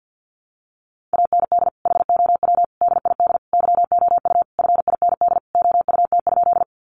Morse code, heard as a keyed tone, about 50 wpm, 700 Hz. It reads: AND HOW DID YOU FIND OUT?
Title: Sentences from Top 100 Words 50wpm